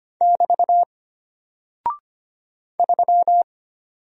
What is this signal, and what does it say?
Morse code, heard as a keyed tone, about 25 wpm, 700 Hz. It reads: <BT> 3
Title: Single Character (Letter:Number:Punctuation:Prosign) Alt 25wpm